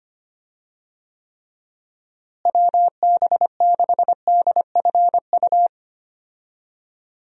Morse code, heard as a keyed tone, about 25 wpm, 700 Hz. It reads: WB6DFU
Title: USA Call Signs - 6 Characters - Repeated 2x - 25wpm